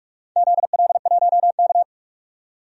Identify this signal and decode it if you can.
Morse code, heard as a keyed tone, about 42 wpm, 700 Hz. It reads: ZL1K
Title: DXCC-100 Call Signs - 4 Characters - 45wpm